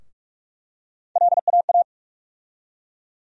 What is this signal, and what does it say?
Morse code, heard as a keyed tone, about 42 wpm, 700 Hz. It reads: LAA
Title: USA Call Sign Postfixes - 3 Letters 45wpm